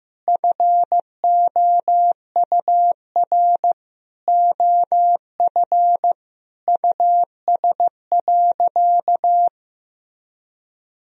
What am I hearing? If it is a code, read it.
Morse code, heard as a keyed tone, about 15 wpm, 700 Hz. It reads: FOUR OF US.
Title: Sentences from Top 1000 Words - Review Part 1 - 15wpm